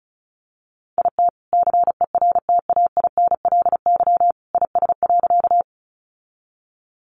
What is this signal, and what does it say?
Morse code, heard as a keyed tone, about 34 wpm, 700 Hz. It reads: IT CERTAINLY IS.
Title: Sentences from Top 900 Words 35wpm